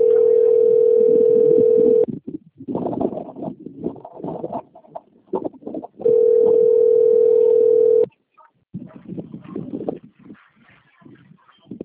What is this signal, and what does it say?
Ringback tone (North America), 440 + 480 Hz.